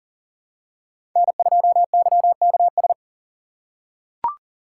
Morse code, heard as a keyed tone, about 40 wpm, 700 Hz. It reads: N2YKS
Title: USA Call Signs - 4x Character Spacing - 40wpm